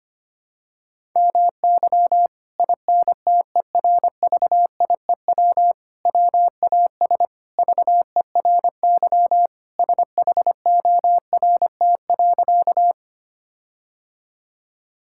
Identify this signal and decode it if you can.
Morse code, heard as a keyed tone, about 25 wpm, 700 Hz. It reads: MY INTERVIEW WAS VERY SHORT.
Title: Sentences from Top 1600 Words 25wpm